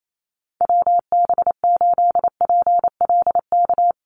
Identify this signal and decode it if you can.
Morse code, heard as a keyed tone, about 28 wpm, 700 Hz. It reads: WB8PLK
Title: USA Call Signs Repeated 2x 28wpm